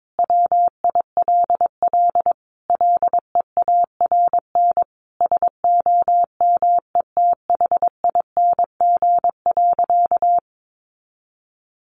Morse code, heard as a keyed tone, about 22 wpm, 700 Hz. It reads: WILL LEARN SOMETHING.
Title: Sentences from Top 600 Words 22wpm